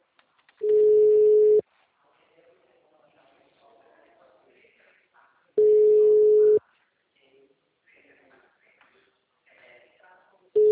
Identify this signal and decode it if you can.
Ringback tone (Europe, 425 Hz), with speech.